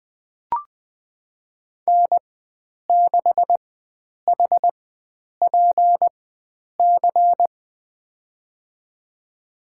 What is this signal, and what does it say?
Morse code, heard as a keyed tone, about 20 wpm, spaced as for 9 wpm, 700 Hz. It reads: N6HPC
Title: USA Call Signs - 5 Characters - 4x Character Spacing - 20wpm